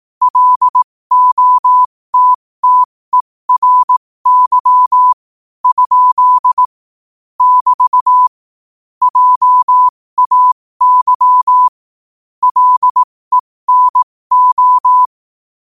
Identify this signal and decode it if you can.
Morse code, heard as a keyed tone, about 18 wpm, spaced as for 15 wpm, 1000 Hz. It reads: LOTTERY? <BT> JAY LENO
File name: 15wpm